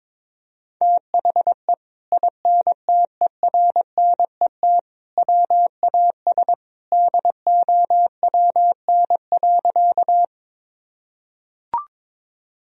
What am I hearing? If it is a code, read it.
Morse code, heard as a keyed tone, about 22 wpm, 700 Hz. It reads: THE INTERNET WAS DOWN.
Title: Sentences from Top 1000 Words - Review Part 4 - 22wpm